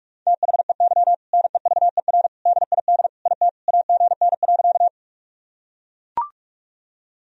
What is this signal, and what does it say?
Morse code, heard as a keyed tone, about 41 wpm, 700 Hz. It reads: THEY NEVER DID IT AGN.
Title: Sentences from Top 300 Words and Abbreviations 45wpm